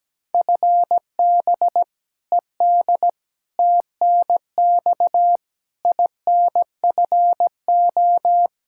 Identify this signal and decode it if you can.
Morse code, heard as a keyed tone, about 17 wpm, 700 Hz. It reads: FB ED TNX INFO
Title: Practice QSO - Exchange 3 - More Info - 17wpm